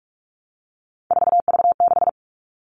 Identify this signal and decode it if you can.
Morse code, heard as a keyed tone, about 44 wpm, 700 Hz. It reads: 4V6
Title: All Cut Numbers - 3-Digit 45wpm